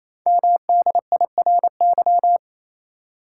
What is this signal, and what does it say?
Morse code, heard as a keyed tone, about 27 wpm, 700 Hz. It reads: MDIRY
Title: Code Groups - 5 Letters - 28wpm